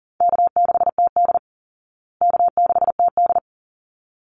Morse code, heard as a keyed tone, about 40 wpm, 700 Hz. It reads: K6TD K6TD
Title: Call Signs Repeated 3x - 40wpm